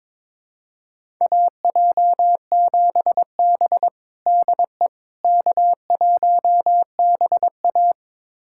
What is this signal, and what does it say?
Morse code, heard as a keyed tone, about 22 wpm, 700 Hz. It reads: AJ7B DE K1BA
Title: USA Callsign DE Callsign - 4 Characters - 22wpm